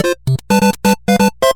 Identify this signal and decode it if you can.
There is no code.